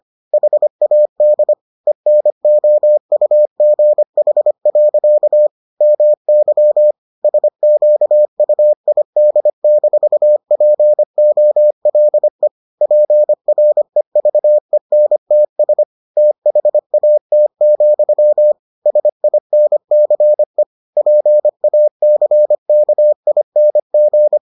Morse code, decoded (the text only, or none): HAD ENOUGH. MY SQUID-POLE PREVENTS THAT, SINCE PACKING